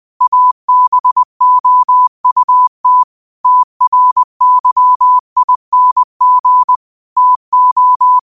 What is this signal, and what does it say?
Morse code, heard as a keyed tone, about 20 wpm, 1000 Hz. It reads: ABOUT TRYING TO
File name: 20wpm